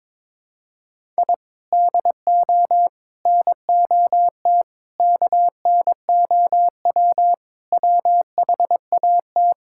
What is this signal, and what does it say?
Morse code, heard as a keyed tone, about 22 wpm, 700 Hz. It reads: I DO NOT KNOW WHAT
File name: practice / 22wpm / Sentences with 4-Word Phrases - 22wpm